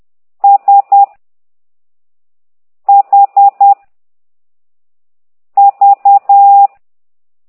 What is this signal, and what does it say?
Morse code, heard as a keyed tone, about 10 wpm, 800 Hz. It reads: S H V
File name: ct2idw / exerc9